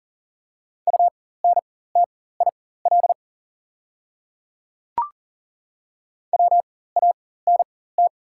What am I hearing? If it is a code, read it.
Morse code, heard as a keyed tone, about 40 wpm, spaced as for 19 wpm, 700 Hz. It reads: UNTIL WANT
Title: 4x Character Spacing - Top 300 Words - 40wpm